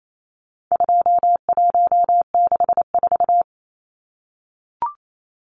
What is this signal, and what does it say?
Morse code, heard as a keyed tone, about 28 wpm, 700 Hz. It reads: 2164
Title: Numbers - 4-Digit 28wpm